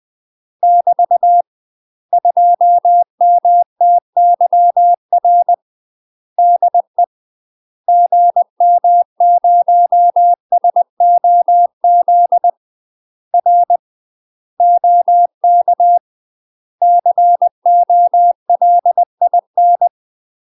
Morse code, heard as a keyed tone, about 20 wpm, 700 Hz. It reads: <BT> 2MTYR DE GM0SOZ R OK COLIN
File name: UK MASTER QSOs TRAINING / 20WPM / UK MASTER QSOs 0063